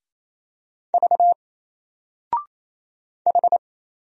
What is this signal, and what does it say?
Morse code, heard as a keyed tone, about 27 wpm, 700 Hz. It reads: V H
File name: Single-Letter-Number-28wpm